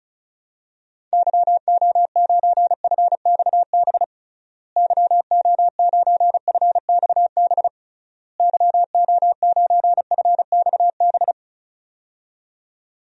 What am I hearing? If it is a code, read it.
Morse code, heard as a keyed tone, about 36 wpm, 700 Hz. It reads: YO9FXB YO9FXB YO9FXB
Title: DXCC-100 Call Signs - 6 Characters - Repeated 3x - 35wpm